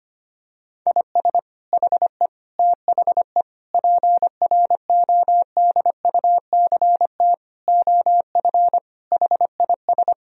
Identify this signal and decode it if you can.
Morse code, heard as a keyed tone, about 25 wpm, 700 Hz. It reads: IS HE THE PRODUCT OF HIS
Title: Sentences from Top 1000 Words - Review Part 2 - 25wpm